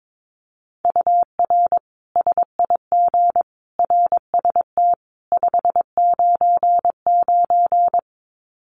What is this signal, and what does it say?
Morse code, heard as a keyed tone, about 22 wpm, 700 Hz. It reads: UR SIG RST 599
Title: Common QSO Phrases 22wpm